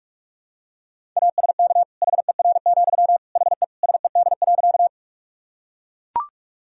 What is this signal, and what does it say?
Morse code, heard as a keyed tone, about 42 wpm, 700 Hz. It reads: ASK HER, HE SED.
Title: Sentences from Top 200 Words and Abbreviations 45wpm